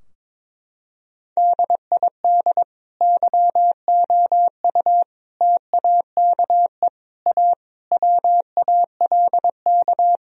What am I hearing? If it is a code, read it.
Morse code, heard as a keyed tone, about 22 wpm, 700 Hz. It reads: DID YOU TAKE A WALK
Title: Sentences from Top 1000 Words - Review Part 4 - 22wpm